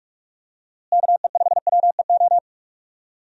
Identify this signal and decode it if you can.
Morse code, heard as a keyed tone, about 42 wpm, 700 Hz. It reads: KE5WEO
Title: USA Call Signs - 6 Characters - Repeated 2x - 45wpm